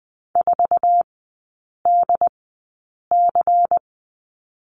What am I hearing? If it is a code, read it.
Morse code, heard as a keyed tone, about 20 wpm, spaced as for 8 wpm, 690 Hz. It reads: 4DC
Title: 145 Lesson 24e - Letter y - Three Characters 20wpm